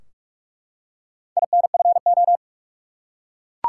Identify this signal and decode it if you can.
Morse code, heard as a keyed tone, about 44 wpm, 700 Hz. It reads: INFO
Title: Top 180 QSO Elements 45wpm